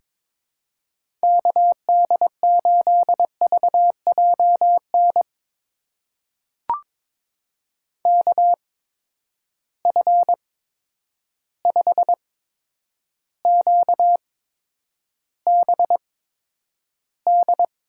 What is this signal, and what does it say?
Morse code, heard as a keyed tone, about 22 wpm, spaced as for 9 wpm, 700 Hz. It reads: KD8VJN KF5QBD
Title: USA Call Signs - 6 Characters - 8x Character Spacing - 22wpm